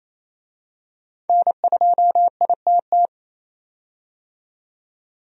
Morse code, heard as a keyed tone, about 28 wpm, 700 Hz. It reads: N2ITT
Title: USA Call Signs - 5 Characters - 28wpm